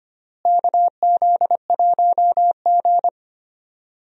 Morse code, heard as a keyed tone, about 25 wpm, 700 Hz. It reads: KZ1G
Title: USA Call Signs - 4 Characters - Repeated 3x - 25wpm